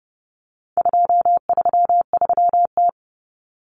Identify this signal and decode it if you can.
Morse code, heard as a keyed tone, about 30 wpm, 700 Hz. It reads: 233T
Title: Common Cut Numbers - 4-Digit 30wpm